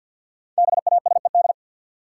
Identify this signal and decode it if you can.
Morse code, heard as a keyed tone, about 49 wpm, 700 Hz. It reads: BASED
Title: Sets of 2 Words - New Word Drill - 50wpm